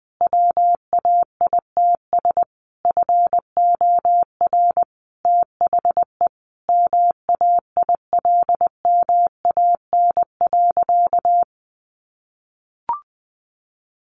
Morse code, heard as a keyed tone, about 20 wpm, 700 Hz. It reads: WAITS FOR THE MAILMAN.
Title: Easy Phrases 20wpm